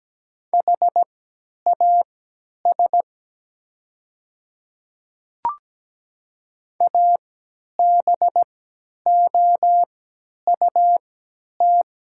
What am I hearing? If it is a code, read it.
Morse code, heard as a keyed tone, about 17 wpm, spaced as for 10 wpm, 700 Hz. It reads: HAS ABOUT
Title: 3x Character Spacing - Top 100 Words - 17wpm